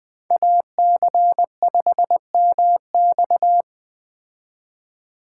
Morse code, heard as a keyed tone, about 20 wpm, 700 Hz. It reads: AC5MX